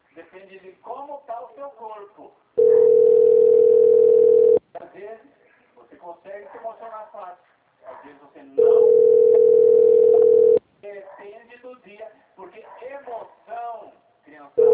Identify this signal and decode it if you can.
Ringback tone (North America), 440 + 480 Hz, with speech.
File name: out- - - - - .7375